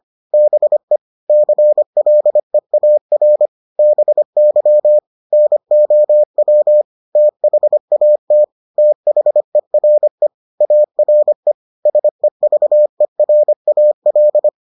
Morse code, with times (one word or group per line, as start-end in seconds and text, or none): 0.3-1.0 BE
1.3-3.5 CLEAR
3.8-5.0 BY
5.3-6.8 NOW
7.2-8.4 THAT
8.8-10.3 THERE
10.6-11.5 ARE
11.9-14.5 SEVERAL